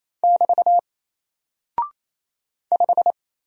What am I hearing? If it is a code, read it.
Morse code, heard as a keyed tone, about 28 wpm, 700 Hz. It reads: <BT> 5
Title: Single Character (Letter:Number:Punctuation:Prosign) Alt 28wpm